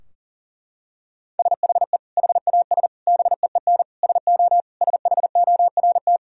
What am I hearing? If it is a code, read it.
Morse code, heard as a keyed tone, about 37 wpm, 700 Hz. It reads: SHE HAS BEEN SO SHORT
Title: Sentences from Top 1000 Words - Review Part 2 - 40wpm